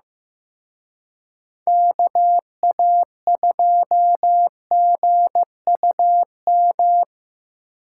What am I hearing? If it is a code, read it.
Morse code, heard as a keyed tone, about 17 wpm, 700 Hz. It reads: KA2GUM
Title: Speed Racing - USA 2x3 Call Signs 15wpm